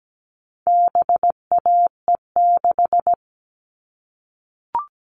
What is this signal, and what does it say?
Morse code, heard as a keyed tone, about 17 wpm, 700 Hz. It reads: BAE6
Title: All Cut Numbers - 4-Digit 17wpm